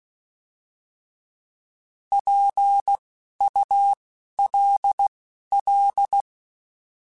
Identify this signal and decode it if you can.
Morse code, heard as a keyed tone, about 16 wpm, spaced as for 11 wpm, 800 Hz. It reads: PULL
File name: Alphabetical / Alphabetical mp3s / Ps @10